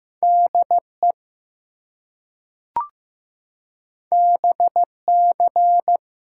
Morse code, heard as a keyed tone, about 15 wpm, 700 Hz. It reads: DE BC
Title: US States and Canadian Provinces 15wpm